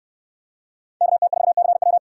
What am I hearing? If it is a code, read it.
Morse code, heard as a keyed tone, about 66 wpm, 700 Hz. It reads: KT4CW